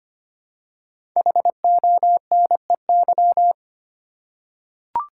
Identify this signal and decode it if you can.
Morse code, heard as a keyed tone, about 25 wpm, 700 Hz. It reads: HONEY
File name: 5 Letter Words - First 3 Letters Given - 25wpm